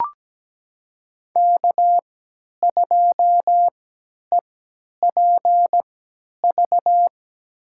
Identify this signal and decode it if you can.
Morse code, heard as a keyed tone, about 17 wpm, spaced as for 10 wpm, 700 Hz. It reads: K2EPV
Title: USA Call Signs - 5 Characters - 3x Character Spacing - 17wpm